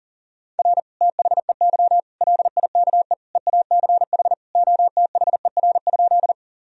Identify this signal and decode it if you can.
Morse code, heard as a keyed tone, about 40 wpm, 700 Hz. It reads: R THEY LIKE EACH OTHER?
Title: Sentences from Top 200 Words and Abbreviations 40wpm